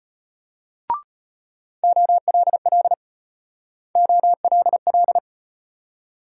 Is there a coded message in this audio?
Yes: Morse code from 1.8 to 5.2 s, heard as a keyed tone, about 37 wpm, 700 Hz. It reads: OLL OLL